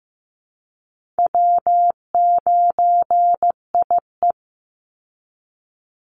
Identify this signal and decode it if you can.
Morse code, heard as a keyed tone, about 15 wpm, 700 Hz. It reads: W9IE
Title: Easy Call Signs Repeated 3x - 15wpm